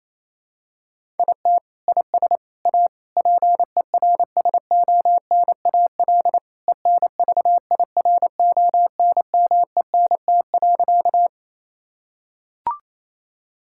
Morse code, heard as a keyed tone, about 28 wpm, 700 Hz. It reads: IT IS A PERSONAL ENVIRONMENT.
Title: Sentences from Top 1000 Words - Review Part 5 - 28wpm